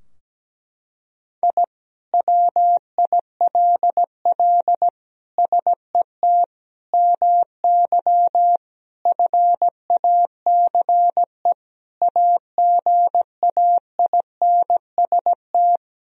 Morse code, heard as a keyed tone, about 17 wpm, 700 Hz. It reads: I WILL SET MY FACE AGAINST
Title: Sentences from Top 1000 Words - Review Part 2 - 17wpm